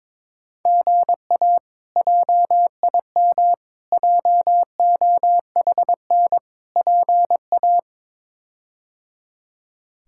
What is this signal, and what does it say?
Morse code, heard as a keyed tone, about 22 wpm, 700 Hz. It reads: GA JIM JOHN PA
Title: K1USN SST QSO Practice - 22wpm